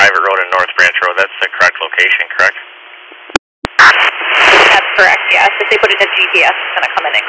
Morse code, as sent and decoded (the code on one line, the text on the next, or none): none